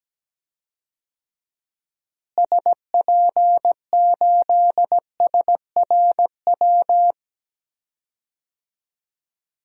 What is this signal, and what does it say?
Morse code, heard as a keyed tone, about 17 wpm, 700 Hz. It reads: SP8SRW
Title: DXCC-100 Call Signs - 6 Characters - Repeated 3x - 17wpm